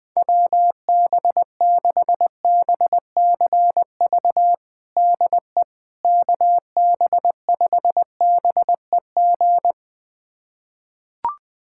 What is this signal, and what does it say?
Morse code, heard as a keyed tone, about 20 wpm, 700 Hz. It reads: WB6BCV DE KB5BEG
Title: USA Callsign DE Callsign - 6 Characters - 20wpm